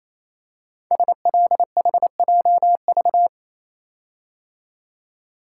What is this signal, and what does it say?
Morse code, heard as a keyed tone, about 29 wpm, 690 Hz. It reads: SLHJV
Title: Code Groups - 5 Letters - 28wpm